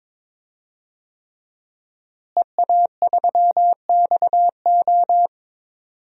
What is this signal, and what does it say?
Morse code, heard as a keyed tone, about 22 wpm, 700 Hz. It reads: EA3XO
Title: DXCC-100 Call Signs - 5 Characters - Repeated 3x - 22wpm